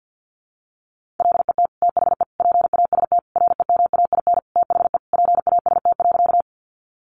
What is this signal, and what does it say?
Morse code, heard as a keyed tone, about 50 wpm, 700 Hz. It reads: LET THE PAST REMAIN THE PAST.
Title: Sentences from Top 1200 Words 50wpm